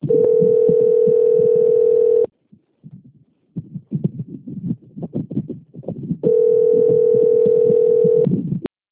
Ringback tone (North America), 440 + 480 Hz.